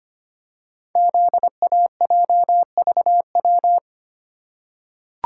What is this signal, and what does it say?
Morse code, heard as a keyed tone, about 25 wpm, 690 Hz. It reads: ZAJVW
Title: Code Groups - 5 Letters - 25wpm